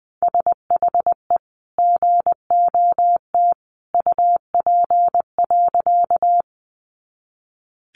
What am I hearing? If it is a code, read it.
Morse code, heard as a keyed tone, about 19 wpm, 700 Hz. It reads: SHE GOT UP.
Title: Sentences from Top 300 Words 20wpm